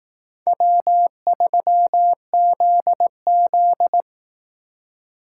Morse code, heard as a keyed tone, about 18 wpm, 700 Hz. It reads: W3ZZ
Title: Speed Racing - USA 1x2 Call Signs 15wpm